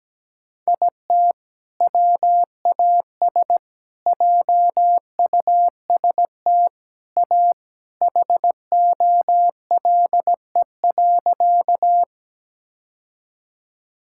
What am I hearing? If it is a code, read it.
Morse code, heard as a keyed tone, about 17 wpm, 700 Hz. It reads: IT WAS JUST A HOLE.
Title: Sentences from Top 1600 Words 17wpm